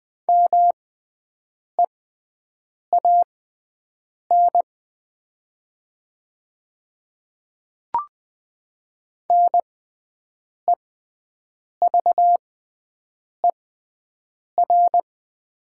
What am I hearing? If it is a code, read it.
Morse code, heard as a keyed tone, about 20 wpm, spaced as for 7 wpm, 690 Hz. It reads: MEAN NEVER
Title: 6x Character Spacing - Top 500 Words - 20wpm